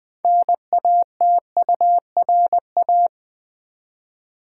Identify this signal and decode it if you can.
Morse code, heard as a keyed tone, about 20 wpm, 700 Hz. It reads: NATURA
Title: Build A Word - 3 Letters At A Time - 9-Letter Words - 20wpm